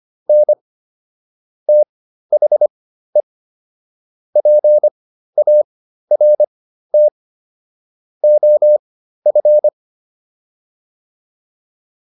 Morse code, heard as a keyed tone, about 25 wpm, spaced as for 13 wpm, 600 Hz. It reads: N THE PART OF
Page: /intermediate-practice-files/